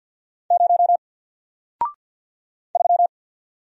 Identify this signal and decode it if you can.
Morse code, heard as a keyed tone, about 49 wpm, 700 Hz. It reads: 0 3
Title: Single Letter-Number Alt 50wpm